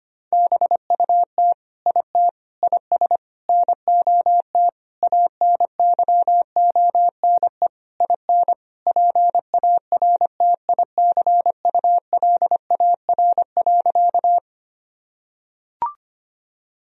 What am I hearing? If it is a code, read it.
Morse code, heard as a keyed tone, about 25 wpm, 700 Hz. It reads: BUT IT IS NOT ANYONE IN PARTICULAR.
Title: Sentences from Top 1000 Words - Review Part 4 - 25wpm